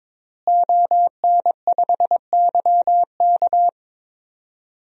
Morse code, heard as a keyed tone, about 22 wpm, 700 Hz. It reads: ON5YK
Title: DXCC-100 Call Signs - 5 Characters - Repeated 2x - 22wpm